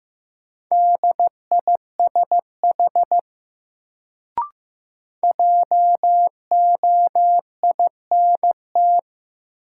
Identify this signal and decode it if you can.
Morse code, heard as a keyed tone, about 15 wpm, 700 Hz. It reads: DISH JOINT
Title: Sentences from Top 1300 Words - New Word Drill - 15wpm